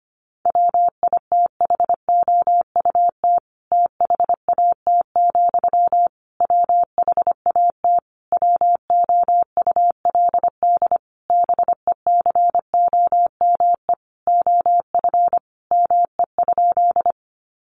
Morse code, heard as a keyed tone, about 25 wpm, 700 Hz. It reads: WITHOUT THAT, WHAT WOULD BECOME OF ME?
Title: Sentences from Top 600 Words 25wpm